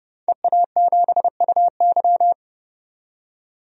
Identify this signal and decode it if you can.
Morse code, heard as a keyed tone, about 30 wpm, 700 Hz. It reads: EA7UY
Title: DXCC-100 Call Signs - 5 Characters - 30wpm